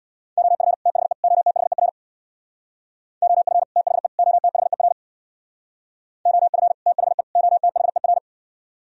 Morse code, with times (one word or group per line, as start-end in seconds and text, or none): 0.4-0.7 OF
0.9-1.1 THE
1.2-1.9 OTHER
3.2-3.6 OF
3.8-4.1 THE
4.2-4.9 OTHER
6.3-6.7 OF
6.9-7.2 THE
7.4-8.2 OTHER